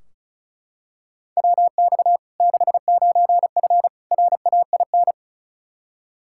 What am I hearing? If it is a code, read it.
Morse code, heard as a keyed tone, about 36 wpm, 700 Hz. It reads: WX 69F RAIN
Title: Practice QSO - Exchange 2 - Info - 35wpm